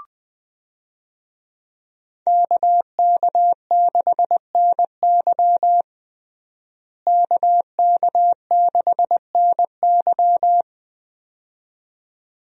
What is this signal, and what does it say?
Morse code, heard as a keyed tone, about 20 wpm, 700 Hz. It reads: KK6NY KK6NY
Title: USA Call Signs - 5 Characters - Repeated 2x - 20wpm